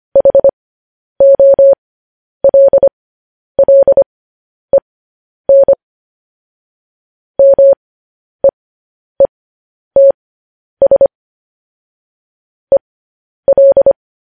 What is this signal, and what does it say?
Morse code, heard as a keyed tone, about 25 wpm, spaced as for 10 wpm, 550 Hz. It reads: HOLLEN MEETS EL
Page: news headlines in morse code at 10 wpm